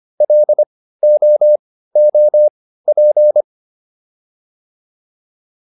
Morse code, heard as a keyed tone, about 25 wpm, spaced as for 15 wpm, 600 Hz. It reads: LOOP